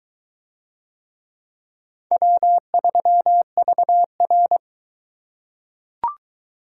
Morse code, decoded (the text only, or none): W3VR